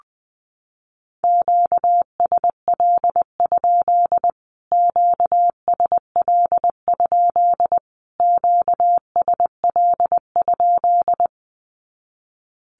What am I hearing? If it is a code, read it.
Morse code, heard as a keyed tone, about 20 wpm, 700 Hz. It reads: QSL? QSL? QSL?
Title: Q-Codes Repeated 3x 20wpm